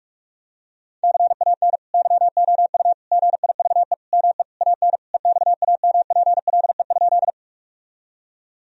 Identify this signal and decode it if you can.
Morse code, heard as a keyed tone, about 46 wpm, 700 Hz. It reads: CAN YOU GIVE ME AN EXAMPLE?
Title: Sentences from Top 1000 Words - Review Part 1 - 45wpm